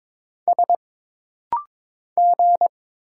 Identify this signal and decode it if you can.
Morse code, heard as a keyed tone, about 22 wpm, 700 Hz. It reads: S G
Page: This rapid-fire timing is geared to push your ICR (Instant Character Recognition) skills.